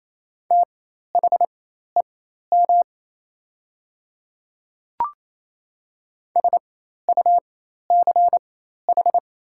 Morse code, heard as a keyed tone, about 27 wpm, spaced as for 13 wpm, 700 Hz. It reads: THEM SUCH
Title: 4x Character Spacing - Top 300 Words - 28wpm